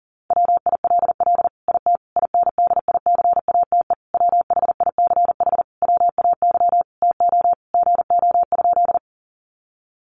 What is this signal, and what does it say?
Morse code, heard as a keyed tone, about 40 wpm, 700 Hz. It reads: WILL IT INDICATE WHICH WAY TO GO?
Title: Sentences from Top 1300 Words 40wpm